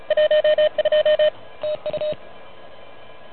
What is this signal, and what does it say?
Morse code, heard as a keyed tone, about 33 wpm, 590 Hz. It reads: 12 TU
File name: truSDX / MWC29.8